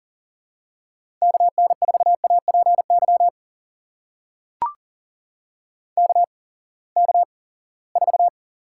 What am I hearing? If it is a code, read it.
Morse code, heard as a keyed tone, about 40 wpm, spaced as for 23 wpm, 700 Hz. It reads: KN4APY KK4
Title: USA Call Signs - 6 Characters - 8x Character Spacing - 40wpm